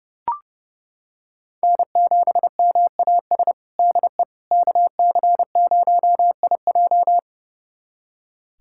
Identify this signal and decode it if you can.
Morse code, heard as a keyed tone, about 30 wpm, 700 Hz. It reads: N7MAS DE KC0IJ
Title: USA Callsign DE Callsign - 5 Characters - 30wpm